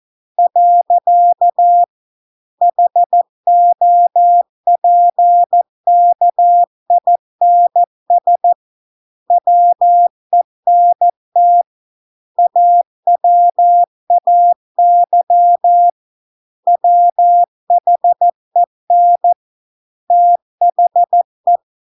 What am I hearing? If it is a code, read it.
Morse code, heard as a keyed tone, about 14 wpm, 700 Hz. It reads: . HOPKINS WENT AWAY WHEN THE